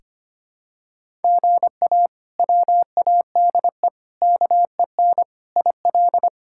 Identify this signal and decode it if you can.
Morse code, heard as a keyed tone, about 25 wpm, 700 Hz. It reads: GA WADE KEN IL
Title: K1USN SST QSO Practice - 25wpm